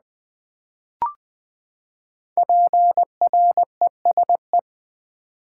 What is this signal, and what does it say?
Morse code, heard as a keyed tone, about 20 wpm, 700 Hz. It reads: PRESE
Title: Build A Word - 5 Letters At A Time - 12-Letter Words - 20wpm